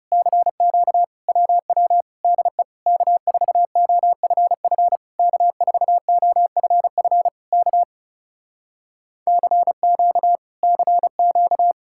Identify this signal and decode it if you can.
Morse code, heard as a keyed tone, about 40 wpm, 700 Hz. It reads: CQ WW DE K4OFF K4OFF K CQ CQ